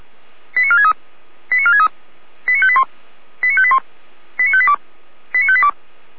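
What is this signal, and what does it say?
Five-tone selective call (ZVEI), heard as quick runs of five tones, tones of about 70 ms. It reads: 89463 89463 89581 89581 89582 89582
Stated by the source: ZVEI five-tone selective call received on SDR.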